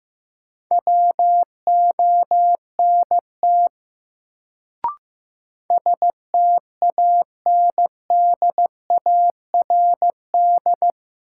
Morse code, heard as a keyed tone, about 15 wpm, 700 Hz. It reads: WONT STANDARD
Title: Sentences from Top 1000 Words - Review Part 6 - New Word Drill - 15wpm